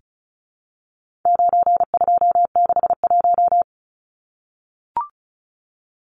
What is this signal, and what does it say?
Morse code, heard as a keyed tone, about 34 wpm, 700 Hz. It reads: 9261
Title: Numbers - 4-Digit 35wpm